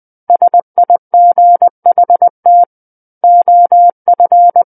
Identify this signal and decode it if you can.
Morse code, heard as a keyed tone, about 20 wpm, 700 Hz. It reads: SIGHT OF
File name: 20 WPM